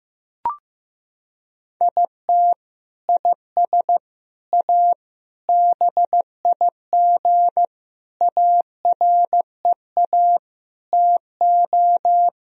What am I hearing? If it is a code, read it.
Morse code, heard as a keyed tone, about 15 wpm, 700 Hz. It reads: IT IS A BIG AREA TO
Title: Sentences from Top 1000 Words - Review Part 2 - 15wpm